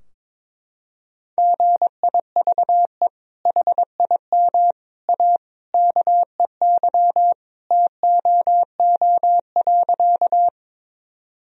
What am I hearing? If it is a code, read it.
Morse code, heard as a keyed tone, about 22 wpm, 700 Hz. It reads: GIVE HIM A KEY TOO.
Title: Sentences from Top 1000 Words - Review Part 4 - 22wpm